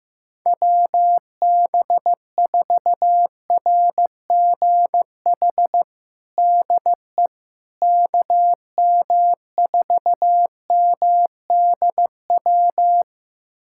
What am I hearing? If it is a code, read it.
Morse code, heard as a keyed tone, about 16 wpm, 700 Hz. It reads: WB4RGH DE KM4MDW
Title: USA Callsign DE Callsign 15wpm